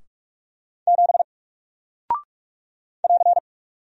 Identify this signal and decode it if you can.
Morse code, heard as a keyed tone, about 44 wpm, 700 Hz. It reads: 7 <AR>
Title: Single Character (Letter:Number:Punctuation:Prosign) Alt 45wpm